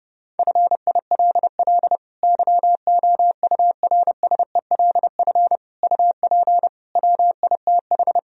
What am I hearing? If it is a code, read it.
Morse code, heard as a keyed tone, about 32 wpm, 700 Hz. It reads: FILL YOURSELF UP WITH
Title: Sentences from Top 1000 Words - Review Part 3 - 30wpm